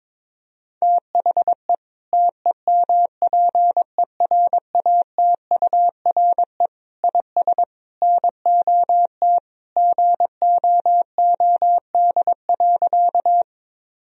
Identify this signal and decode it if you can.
Morse code, heard as a keyed tone, about 22 wpm, 700 Hz. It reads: THE TEMPERATURE IS NOT GOOD.
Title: Sentences from Top 1000 Words - Review Part 4 - 22wpm